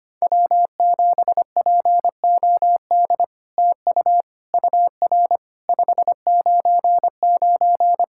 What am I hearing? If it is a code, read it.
Morse code, heard as a keyed tone, about 25 wpm, 700 Hz. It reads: W7POD TU UR 599
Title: SOTA QSO Practice 25wpm